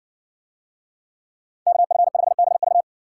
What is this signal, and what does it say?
Morse code, heard as a keyed tone, about 64 wpm, 700 Hz. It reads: KF5BV